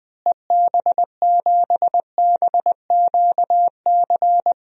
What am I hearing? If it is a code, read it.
Morse code, heard as a keyed tone, about 20 wpm, 700 Hz. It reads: EB7BQC
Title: DXCC-100 Call Signs - 6 Characters - Repeated 2x - 20wpm